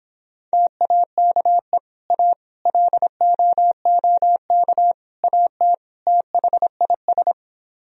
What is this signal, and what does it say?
Morse code, heard as a keyed tone, about 26 wpm, 700 Hz. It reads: TAKE A LOOK AT THIS